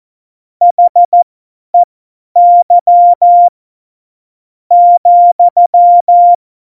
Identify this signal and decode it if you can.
Morse code, heard as a keyed tone, about 14 wpm, spaced as for 8 wpm, 700 Hz. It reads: HEY,
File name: Morsekurs / Witze / Witz 25. September 2018, 50 F70